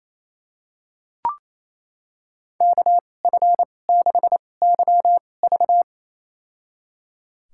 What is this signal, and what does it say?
Morse code, heard as a keyed tone, about 28 wpm, spaced as for 20 wpm, 700 Hz. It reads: KF6YV
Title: USA Call Signs - 2x Character Spacing - 28wpm